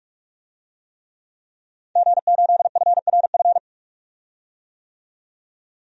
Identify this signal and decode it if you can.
Morse code, heard as a keyed tone, about 46 wpm, 700 Hz. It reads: G8FRF
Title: DXCC-100 Call Signs - 5 Characters - 45wpm